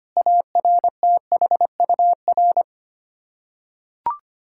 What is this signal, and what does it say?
Morse code, heard as a keyed tone, about 25 wpm, 700 Hz. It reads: ARTHUR
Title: Practice QSO - Exchange 1 - Beginning - 25wpm